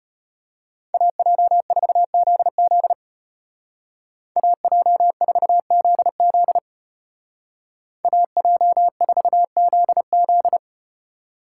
Morse code, heard as a keyed tone, about 34 wpm, 700 Hz. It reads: AJ4ZZ AJ4ZZ AJ4ZZ